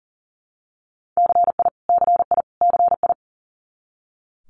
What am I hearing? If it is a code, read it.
Morse code, heard as a keyed tone, about 41 wpm, 700 Hz. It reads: CI CI CI
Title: Multi-letter Phonemes Repeated 3x 40wpm